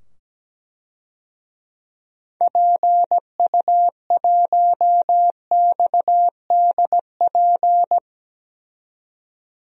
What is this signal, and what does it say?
Morse code, heard as a keyed tone, about 17 wpm, 700 Hz. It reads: PU1XDP
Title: DXCC-100 Call Signs - 6 Characters - 17wpm